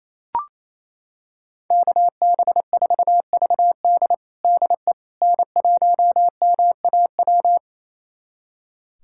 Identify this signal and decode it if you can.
Morse code, heard as a keyed tone, about 28 wpm, 700 Hz. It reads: KB4VD DE N1MAW
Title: USA Callsign DE Callsign - 5 Characters - 28wpm